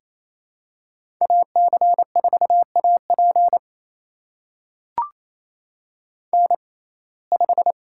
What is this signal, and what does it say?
Morse code, heard as a keyed tone, about 28 wpm, spaced as for 20 wpm, 700 Hz. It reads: AC4AP N5
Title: USA Call Signs - 5 Characters - 6x Character Spacing - 28wpm